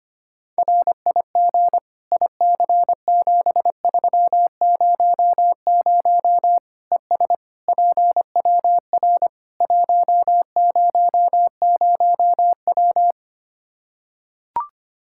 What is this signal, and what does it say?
Morse code, heard as a keyed tone, about 25 wpm, 700 Hz. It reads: RIG IC7300 ES PWR 100W
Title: Practice QSO - Exchange 2 - Info - 25wpm